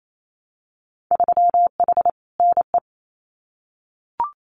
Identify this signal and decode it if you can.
Morse code, heard as a keyed tone, about 28 wpm, 700 Hz. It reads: 3H NE
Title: Winter Field Day Exchanges Encore 28wpm